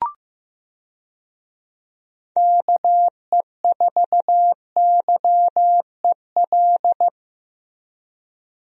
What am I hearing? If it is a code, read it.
Morse code, heard as a keyed tone, about 15 wpm, 700 Hz. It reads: KE4YEL
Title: USA Call Signs - 6 Characters - Repeated 3x - 15wpm